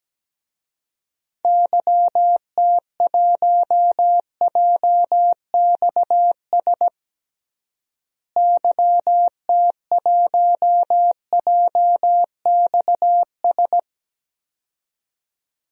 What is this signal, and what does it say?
Morse code, heard as a keyed tone, about 17 wpm, 700 Hz. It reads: YT1JXS YT1JXS
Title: DXCC-100 Call Signs - 6 Characters - Repeated 2x - 17wpm